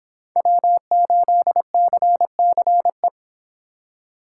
Morse code, heard as a keyed tone, about 26 wpm, 700 Hz. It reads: W8CCE